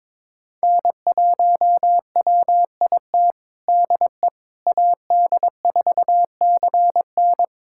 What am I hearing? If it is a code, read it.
Morse code, heard as a keyed tone, about 21 wpm, 700 Hz. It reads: N1WIT DE AD4CN
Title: USA Callsign DE Callsign - 5 Characters - 22wpm